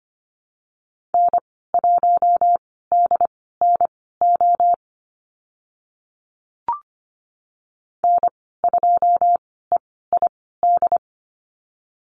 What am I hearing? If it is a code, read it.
Morse code, heard as a keyed tone, about 25 wpm, spaced as for 16 wpm, 700 Hz. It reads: N1DNO N2EID
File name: Lesson 15g - Number 5 - Call Signs 25wpm